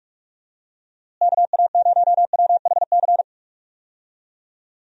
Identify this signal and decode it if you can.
Morse code, heard as a keyed tone, about 46 wpm, 700 Hz. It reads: KA0WHC
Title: USA Call Signs - 6 Characters - Repeated 2x - 45wpm